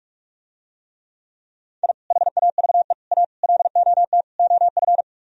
Morse code, heard as a keyed tone, about 47 wpm, 700 Hz. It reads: I HAVE A LOT OF